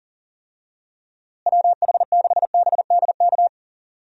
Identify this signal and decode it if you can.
Morse code, heard as a keyed tone, about 40 wpm, 700 Hz. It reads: WH6BDK
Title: USA Call Signs - 6 Characters - Repeated 2x - 40wpm